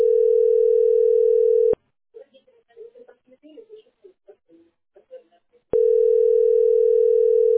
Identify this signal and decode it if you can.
Ringback tone (North America), 440 + 480 Hz, with speech.
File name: db681ef6-7324-43b6-9084